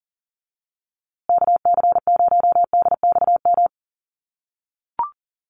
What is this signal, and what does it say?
Morse code, heard as a keyed tone, about 40 wpm, 700 Hz. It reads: KC0DXM
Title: USA Call Signs Repeated 2x 40wpm